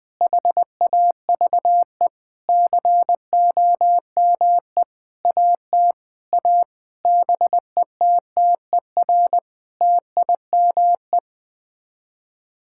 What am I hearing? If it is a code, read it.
Morse code, heard as a keyed tone, about 21 wpm, 700 Hz. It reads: HAVE COME AT A BETTER TIME
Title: Sets of 8 Words - 20wpm